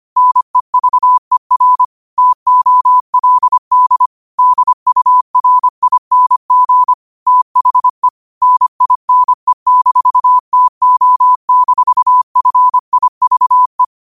Morse code, heard as a keyed tone, about 25 wpm, 1000 Hz. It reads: NEVER TOLD DURING THE NINE-TO-FIVE
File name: qotdfiles / 25wpm